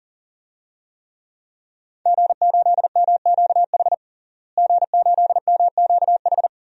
Morse code, heard as a keyed tone, about 37 wpm, 700 Hz. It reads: G8MQH G8MQH
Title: DXCC-100 Call Signs - 5 Characters - Repeated 3x - 40wpm